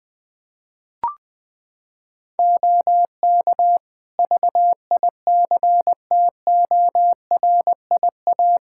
Morse code, heard as a keyed tone, about 21 wpm, 700 Hz. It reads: OK VICTORIA
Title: Practice QSO - Exchange 3 - More Info - 20wpm